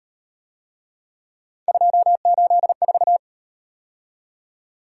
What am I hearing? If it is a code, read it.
Morse code, heard as a keyed tone, about 37 wpm, 700 Hz. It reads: 284